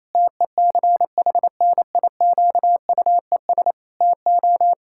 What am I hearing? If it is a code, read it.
Morse code, heard as a keyed tone, about 28 wpm, 700 Hz. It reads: TECHNIQUES TO
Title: Pangram Challenge v2 - 28wpm